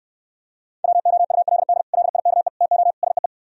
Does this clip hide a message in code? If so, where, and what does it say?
Yes: Morse code from 0.8 to 3.3 s, heard as a keyed tone, about 68 wpm, 700 Hz. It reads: WOULD LIKE TO SEE